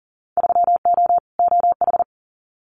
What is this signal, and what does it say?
Morse code, heard as a keyed tone, about 40 wpm, 700 Hz. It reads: 3O OH
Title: Winter Field Day Exchanges 40wpm